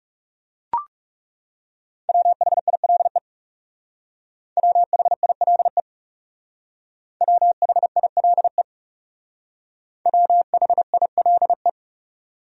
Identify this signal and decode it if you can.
Morse code, heard as a keyed tone, about 37 wpm, 700 Hz. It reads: WHILE WHILE WHILE WHILE
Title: Speed-Racing Top 300 Words 30wpm